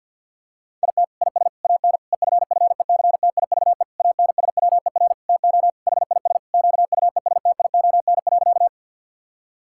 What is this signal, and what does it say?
Morse code, heard as a keyed tone, about 50 wpm, 700 Hz. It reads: IT IS AN EFFECTIVE ANSWER TO HIS QUESTION.
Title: Sentences from Top 1000 Words - Review Part 6 - 50wpm